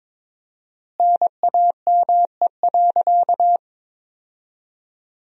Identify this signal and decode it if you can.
Morse code, heard as a keyed tone, about 23 wpm, 700 Hz. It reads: NAME.
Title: 4x Word-Spacing - Sentences from Top 500 Words - 22wpm